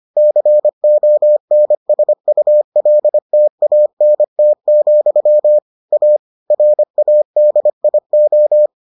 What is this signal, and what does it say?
Morse code, heard as a keyed tone, about 25 wpm, 600 Hz. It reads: CONSULTANT, A RADIO